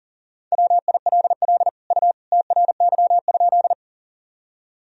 Morse code, heard as a keyed tone, about 37 wpm, 700 Hz. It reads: WILL U TRY?
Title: Sentences from Top 200 Words and Abbreviations 40wpm